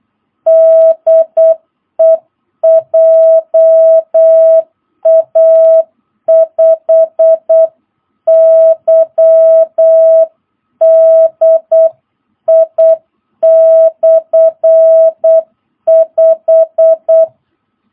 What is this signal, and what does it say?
Morse code, heard as a keyed tone, about 8 wpm, 660 Hz. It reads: DE JA5YDI/5